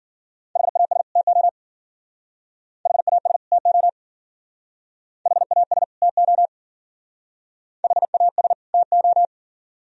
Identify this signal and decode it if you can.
Morse code, heard as a keyed tone, about 51 wpm, 700 Hz. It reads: HAS TO HAS TO HAS TO HAS TO